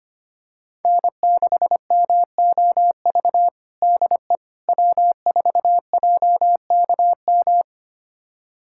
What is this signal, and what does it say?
Morse code, heard as a keyed tone, about 25 wpm, 700 Hz. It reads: N6MOV DE W4JKM
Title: USA Callsign DE Callsign - 5 Characters - 25wpm